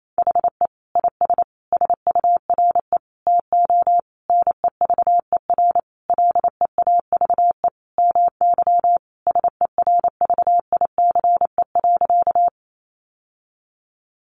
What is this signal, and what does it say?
Morse code, heard as a keyed tone, about 28 wpm, 700 Hz. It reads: HE IS SURE TO NEVER LEAVE MY SERVICE.
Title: Sentences from Top 700 Words 28wpm